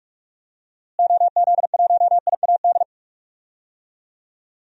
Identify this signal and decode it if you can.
Morse code, heard as a keyed tone, about 45 wpm, 700 Hz. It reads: OZ1IAD
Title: DXCC-100 Call Signs - 6 Characters - Repeated 2x - 45wpm